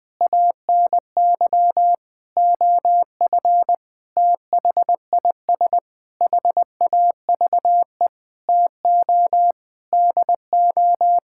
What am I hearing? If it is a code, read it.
Morse code, heard as a keyed tone, about 20 wpm, 700 Hz. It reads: ANY OF THIS HAVE TO DO
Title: Sets of 9 Words - 20wpm